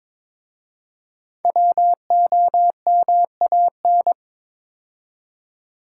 Morse code, heard as a keyed tone, about 22 wpm, 700 Hz. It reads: WOMAN
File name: 5 Letter Words - First 3 Letters Given - 22wpm